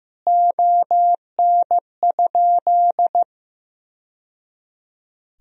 Morse code, heard as a keyed tone, about 16 wpm, 700 Hz. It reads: ON?
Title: Sentences from Top 1000 Words - Review Part 4 - 15wpm